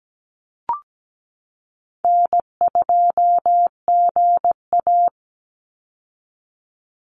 Morse code, heard as a keyed tone, about 17 wpm, 700 Hz. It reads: N2GA
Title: Call Signs - 17wpm